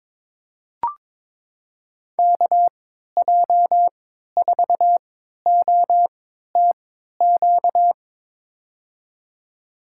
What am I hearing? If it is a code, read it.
Morse code, heard as a keyed tone, about 22 wpm, spaced as for 13 wpm, 700 Hz. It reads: KJ4OTQ
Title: USA Call Signs - 6 Characters - 3x Character Spacing - 22wpm